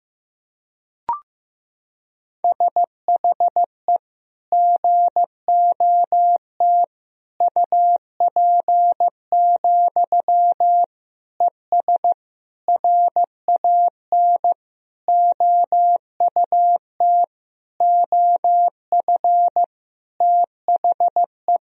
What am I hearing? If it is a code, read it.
Morse code, heard as a keyed tone, about 15 wpm, 700 Hz. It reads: SHE GOT UP, ES RAN OUT OF THE
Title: Sentences from Top 500 Words and Abbreviations 15wpm